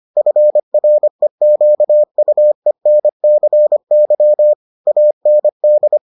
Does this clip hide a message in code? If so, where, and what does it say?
Yes: Morse code from 0.2 to 6.0 s, heard as a keyed tone, about 25 wpm, 600 Hz. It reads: FREQUENCY AND